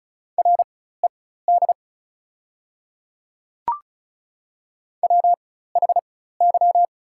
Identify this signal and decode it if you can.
Morse code, heard as a keyed tone, about 36 wpm, spaced as for 16 wpm, 690 Hz. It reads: RED WHY
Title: 4x Character Spacing - Top 500 Words - 35wpm